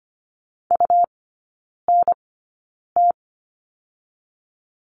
Morse code, heard as a keyed tone, about 25 wpm, spaced as for 9 wpm, 700 Hz. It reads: UNT
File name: Lesson 17e - Letter u - Three Characters 25wpm